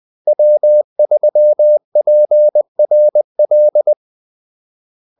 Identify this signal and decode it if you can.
Morse code, heard as a keyed tone, about 20 wpm, 600 Hz. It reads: W3PRL
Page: Kaldesignaler 20 wpm